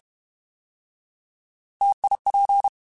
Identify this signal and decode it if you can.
Morse code, heard as a keyed tone, about 32 wpm, 800 Hz. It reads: TIP
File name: Alphabetical / Alphabetical mp3s / Ts @30